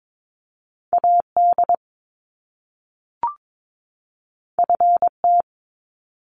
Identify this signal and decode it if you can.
Morse code, heard as a keyed tone, about 22 wpm, 700 Hz. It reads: AD FT
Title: Top Two Letter Combinations in Words 22wpm